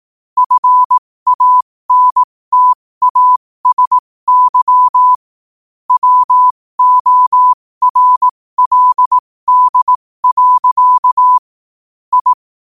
Morse code, heard as a keyed tone, about 18 wpm, spaced as for 15 wpm, 1000 Hz. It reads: FANTASY WORLD. I
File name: qotdfiles / 15wpm